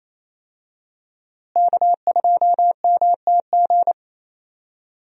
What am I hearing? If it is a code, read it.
Morse code, heard as a keyed tone, about 27 wpm, 700 Hz. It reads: K2MTG
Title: USA Call Signs - 5 Characters - Repeated 3x - 28wpm